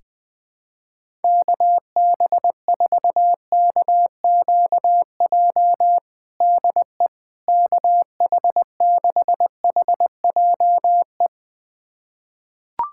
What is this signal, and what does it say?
Morse code, heard as a keyed tone, about 21 wpm, 700 Hz. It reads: KB4KQJ DE KH6HJE
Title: USA Callsign DE Callsign 20wpm